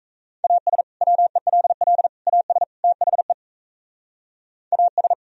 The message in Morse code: AS WELL AS THE AS

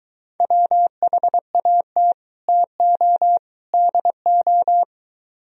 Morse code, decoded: WHAT TO DO

